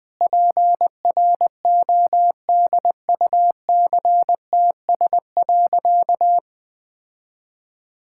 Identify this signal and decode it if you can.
Morse code, heard as a keyed tone, about 19 wpm, 700 Hz. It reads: PRODUCTS.
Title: Sentences with 6-Word Phrases - 20wpm